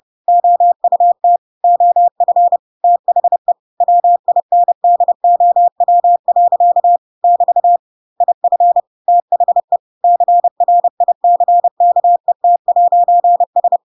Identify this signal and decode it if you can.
Morse code, heard as a keyed tone, about 30 wpm, 700 Hz. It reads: OUT OF THE WINDOW. <BT> IF THE CRICKET'S